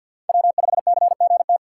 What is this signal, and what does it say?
Morse code, heard as a keyed tone, about 51 wpm, 700 Hz. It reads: W5CGT
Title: USA Call Signs - 5 Characters - 4x Character Spacing - 50wpm